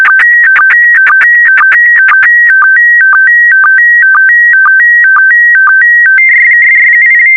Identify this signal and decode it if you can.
SSTV (slow-scan television), PD 120.
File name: sstv / PD120-VK4MSL